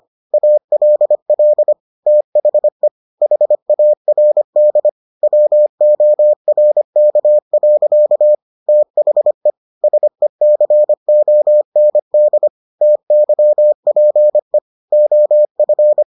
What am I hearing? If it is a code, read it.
Morse code, heard as a keyed tone, about 25 wpm, 600 Hz. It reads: ALL THE HARD WORK. THE SECOND TYPE OF